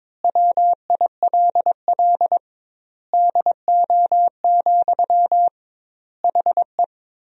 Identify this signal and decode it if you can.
Morse code, heard as a keyed tone, about 22 wpm, 700 Hz. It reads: WILL DO, HE
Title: 2x Word-Spacing - Sentences from Top 100 Words - 22wpm